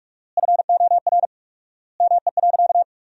Morse code, heard as a keyed tone, about 47 wpm, 700 Hz. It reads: FOR ME.
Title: 4x Word-Spacing - Sentences from Top 300 Words - 45wpm